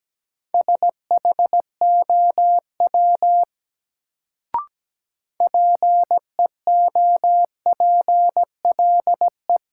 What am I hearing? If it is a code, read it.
Morse code, heard as a keyed tone, about 17 wpm, 700 Hz. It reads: SHOW PEOPLE
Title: Sentences from Top 200 Words - New Word Drill - 17wpm